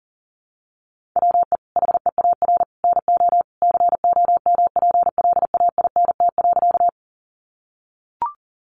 Morse code, heard as a keyed tone, about 40 wpm, 700 Hz. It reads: WE HEAR NO COMPLAINT.
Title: Sentences from Top 1400 Words 40wpm